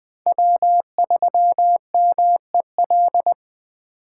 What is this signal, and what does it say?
Morse code, heard as a keyed tone, about 20 wpm, 700 Hz. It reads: W3MEL
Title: K1USN SST QSO Practice - 20wpm